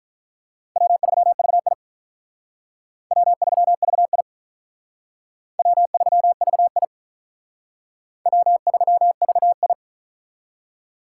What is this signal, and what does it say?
Morse code, heard as a keyed tone, about 44 wpm, 700 Hz. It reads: W3VI W3VI W3VI W3VI